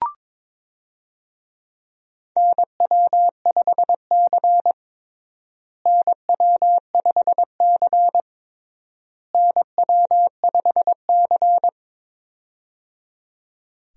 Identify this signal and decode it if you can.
Morse code, heard as a keyed tone, about 22 wpm, 700 Hz. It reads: NW5C NW5C NW5C
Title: USA Call Signs - 4 Characters - Repeated 3x - 22wpm